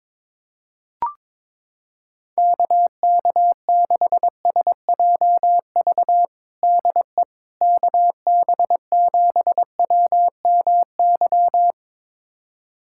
Morse code, heard as a keyed tone, about 22 wpm, 700 Hz. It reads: KK6SJV DE KB7WMY
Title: USA Callsign DE Callsign - 6 Characters - 22wpm